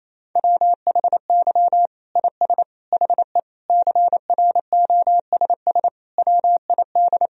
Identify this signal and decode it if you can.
Morse code, heard as a keyed tone, about 28 wpm, 700 Hz. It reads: WHY IS HE CROSS WID
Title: Sentences from Top 300 Words and Abbreviations 28wpm